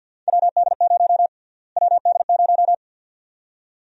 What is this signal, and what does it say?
Morse code, heard as a keyed tone, about 50 wpm, 700 Hz. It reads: WD0 WD0
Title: USA Call Sign Prefixes - Repeated 2x - 50wpm